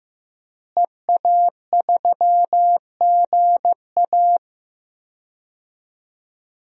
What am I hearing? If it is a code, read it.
Morse code, heard as a keyed tone, about 16 wpm, 700 Hz. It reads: EA3GA
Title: DXCC-100 Call Signs - 5 Characters - Repeated 3x - 15wpm